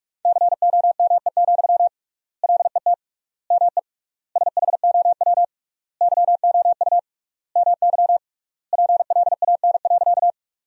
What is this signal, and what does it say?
Morse code, heard as a keyed tone, about 47 wpm, 700 Hz. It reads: COME, LET ME SHOW YOU MY PLAN.
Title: 3x Word-Spacing - Sentences from Top 500 Words - 45wpm